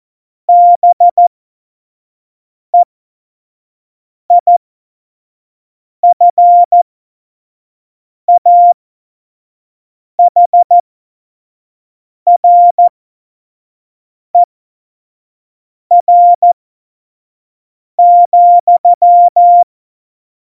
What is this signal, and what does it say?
Morse code, heard as a keyed tone, about 14 wpm, spaced as for 5 wpm, 700 Hz. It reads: BEIFAHRER,
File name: Morsekurs / Witze / Witz 02 Juni 2020, 25 F70